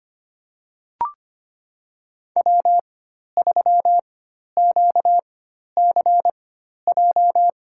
Morse code, heard as a keyed tone, about 25 wpm, spaced as for 12 wpm, 700 Hz. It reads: W3QCJ
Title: USA Call Signs - 5 Characters - 4x Character Spacing - 25wpm